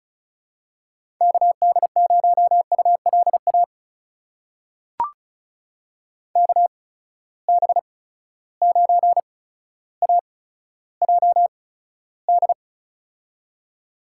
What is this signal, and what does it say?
Morse code, heard as a keyed tone, about 36 wpm, spaced as for 15 wpm, 700 Hz. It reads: KD0ULA KB9AJD
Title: USA Call Signs - 6 Characters - 8x Character Spacing - 35wpm